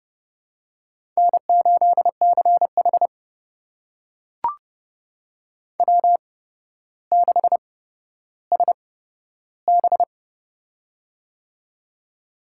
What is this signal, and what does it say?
Morse code, heard as a keyed tone, about 30 wpm, spaced as for 13 wpm, 700 Hz. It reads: N8CH W6SB
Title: USA Call Signs - 4 Characters - 8x Character Spacing - 30wpm